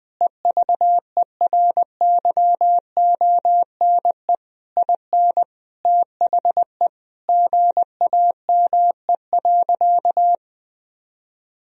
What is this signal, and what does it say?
Morse code, heard as a keyed tone, about 20 wpm, 700 Hz. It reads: EVERYONE IN THE GAME.
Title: Sentences from Top 1000 Words - Review Part 4 - 20wpm